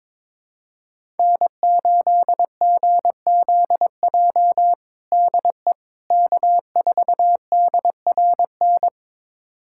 Morse code, heard as a keyed tone, about 22 wpm, 700 Hz. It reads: N8GZJ DE K4DRN
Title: USA Callsign DE Callsign 22wpm